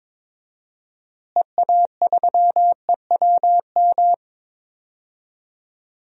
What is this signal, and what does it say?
Morse code, heard as a keyed tone, about 22 wpm, 700 Hz. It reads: EA3EWM
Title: DXCC-100 Call Signs - 6 Characters - Repeated 3x - 22wpm